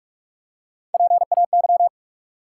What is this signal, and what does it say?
Morse code, heard as a keyed tone, about 47 wpm, 700 Hz. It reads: PAY
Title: Three Letter Words - 45wpm